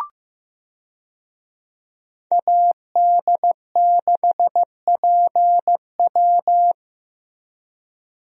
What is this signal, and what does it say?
Morse code, heard as a keyed tone, about 15 wpm, 700 Hz. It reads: AD6PW
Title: USA Call Signs - 5 Characters - Repeated 2x - 15wpm